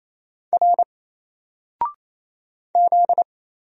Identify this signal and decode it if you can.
Morse code, heard as a keyed tone, about 27 wpm, 700 Hz. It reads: R Z
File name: Single-Letter-Number-28wpm